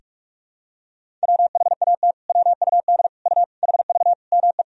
Morse code, heard as a keyed tone, about 46 wpm, 700 Hz. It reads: WHAT WUD U HV ME
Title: Sentences from Top 200 Words and Abbreviations 45wpm